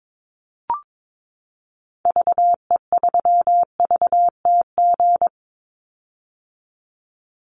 Morse code, heard as a keyed tone, about 22 wpm, 700 Hz. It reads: VE3VTG
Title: Call Signs - 22wpm